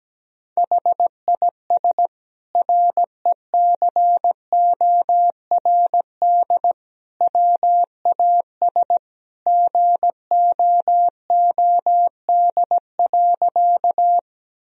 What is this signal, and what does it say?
Morse code, heard as a keyed tone, about 17 wpm, 700 Hz. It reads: HIS RECORD WAS GOOD.
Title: Sentences from Top 1000 Words - Review Part 3 - 17wpm